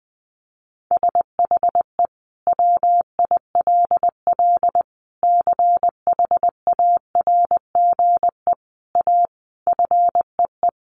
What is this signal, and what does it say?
Morse code, heard as a keyed tone, about 20 wpm, 700 Hz. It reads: SHE WILL CHARGE A FEE
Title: Sentences from Top 1500 Words 20wpm